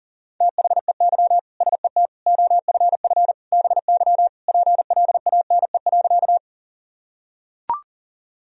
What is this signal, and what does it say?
Morse code, heard as a keyed tone, about 40 wpm, 700 Hz. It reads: THEY SET OFF BY PLANE.
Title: Sentences from Top 1000 Words - Review Part 3 - 40wpm